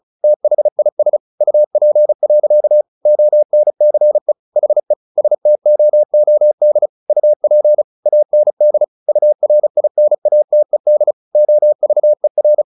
Morse code, heard as a keyed tone, about 34 wpm, 600 Hz. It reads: THIS UP. ONCE HE STOOD UP AND URINATED OVER
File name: the old man and the sea / the old man and the sea 1 35wpm